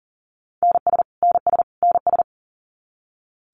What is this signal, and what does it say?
Morse code, heard as a keyed tone, about 40 wpm, 700 Hz. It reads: NS NS NS
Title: Canadian Provinces and Territories - Repeated 3x - 40wpm